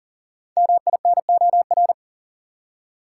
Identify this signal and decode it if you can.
Morse code, heard as a keyed tone, about 40 wpm, 700 Hz. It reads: MINOR
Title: Five Letter Words - 40wpm